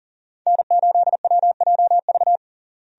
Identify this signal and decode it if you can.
Morse code, heard as a keyed tone, about 40 wpm, 700 Hz. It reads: N8WJV
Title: USA Call Signs - 5 Characters - Repeated 3x - 40wpm